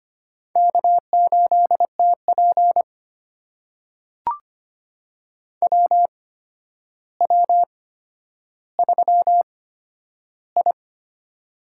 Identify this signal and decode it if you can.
Morse code, heard as a keyed tone, about 25 wpm, spaced as for 11 wpm, 700 Hz. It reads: K8TP WW3I
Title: USA Call Signs - 4 Characters - 8x Character Spacing - 25wpm